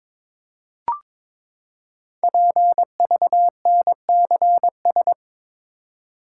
Morse code, heard as a keyed tone, about 22 wpm, 700 Hz. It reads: PVNCS
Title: Code Groups - 5 Letter:Numbers - 22wpm